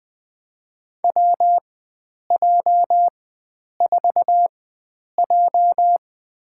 Morse code, heard as a keyed tone, about 21 wpm, spaced as for 9 wpm, 700 Hz. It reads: WJ4J
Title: USA Call Signs - 4 Characters - 4x Character Spacing - 20wpm